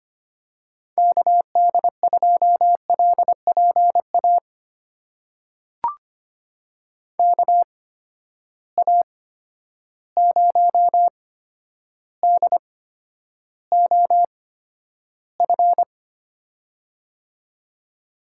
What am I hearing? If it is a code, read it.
Morse code, heard as a keyed tone, about 25 wpm, spaced as for 11 wpm, 700 Hz. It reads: KD2LPA KA0DOF
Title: USA Call Signs - 6 Characters - 8x Character Spacing - 25wpm